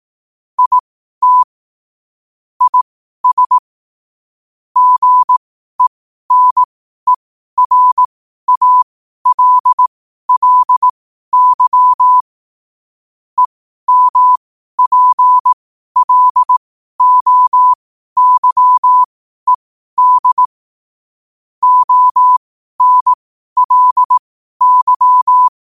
Morse code, heard as a keyed tone, about 18 wpm, spaced as for 12 wpm, 1000 Hz. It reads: IT IS GENERALLY EMPLOYED ONLY
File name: qotdfiles / 12wpm